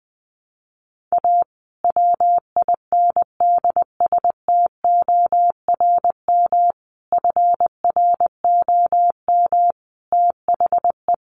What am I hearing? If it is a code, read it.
Morse code, heard as a keyed tone, about 20 wpm, 700 Hz. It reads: A WINDSTORM FROM THE
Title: A Sound Of Thunder 20wpm